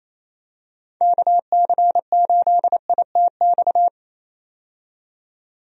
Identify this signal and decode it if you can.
Morse code, heard as a keyed tone, about 28 wpm, 700 Hz. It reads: KC8ITX
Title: USA Call Signs - 6 Characters - 28wpm